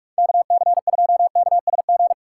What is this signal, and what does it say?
Morse code, heard as a keyed tone, about 46 wpm, 690 Hz. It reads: KC2KSG
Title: USA Call Signs - 6 Characters - 2x Character Spacing - 45wpm